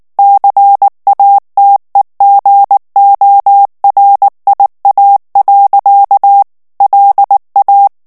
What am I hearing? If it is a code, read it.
Morse code, heard as a keyed tone, about 19 wpm, 800 Hz. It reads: CATEGORIA. LA
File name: morse mp3 / ITALIEN texte clair 2 vitesse 18 mots minute